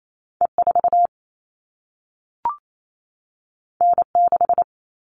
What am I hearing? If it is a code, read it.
Morse code, heard as a keyed tone, about 28 wpm, 700 Hz. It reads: E4 N6
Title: All Cut Numbers - 2-Digit 28wpm